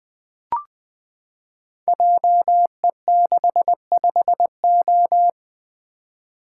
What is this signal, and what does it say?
Morse code, heard as a keyed tone, about 20 wpm, 700 Hz. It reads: JE65O
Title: Code Groups - 5 Letter:Numbers - 20wpm